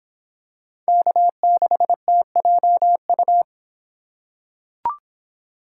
Morse code, heard as a keyed tone, about 26 wpm, 700 Hz. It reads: K6TJU